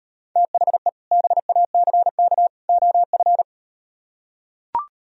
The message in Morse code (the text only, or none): THE BACK OF